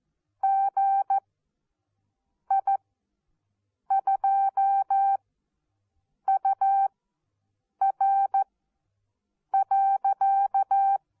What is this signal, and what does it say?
Morse code, heard as a keyed tone, about 14 wpm, spaced as for 6 wpm, 780 Hz. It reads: GI2UR.
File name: Onlinelehrgang / morsen / DJ4UF Morsen14